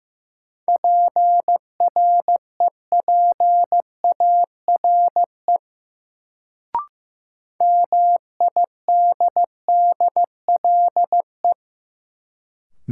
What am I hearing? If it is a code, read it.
Morse code, heard as a keyed tone, about 15 wpm, 700 Hz. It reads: PREPARE MIDDLE
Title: Sentences from Top 900 Words - New Word Drill - 15wpm